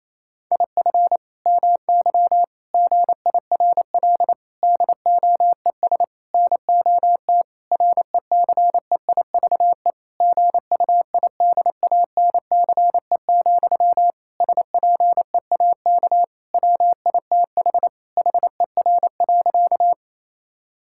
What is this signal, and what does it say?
Morse code, heard as a keyed tone, about 28 wpm, 700 Hz. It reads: IF MY GIRL DOES NOT RECEIVE GUIDANCE, SPEAK WITH HER.
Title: Sentences from Top 1600 Words 28wpm